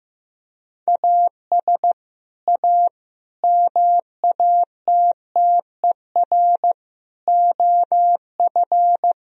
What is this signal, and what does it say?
Morse code, heard as a keyed tone, about 15 wpm, 700 Hz. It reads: AS A MATTER OF
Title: Sentences from Top 1000 Words - Review Part 4 - 15wpm